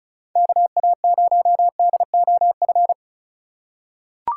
Morse code, heard as a keyed tone, about 36 wpm, 700 Hz. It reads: KA0DOF
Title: USA Call Signs - 6 Characters - 3x Character Spacing - 35wpm